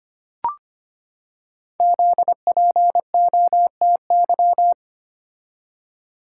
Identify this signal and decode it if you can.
Morse code, heard as a keyed tone, about 25 wpm, 700 Hz. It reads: ZPOTY
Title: Code Groups - 5 Letters - 25wpm